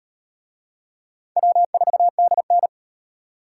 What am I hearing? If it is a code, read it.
Morse code, heard as a keyed tone, about 35 wpm, 700 Hz. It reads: W4DN